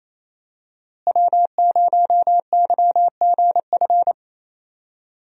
Morse code, heard as a keyed tone, about 28 wpm, 700 Hz. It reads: W0YGF
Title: USA Call Signs - 5 Characters - Repeated 2x - 28wpm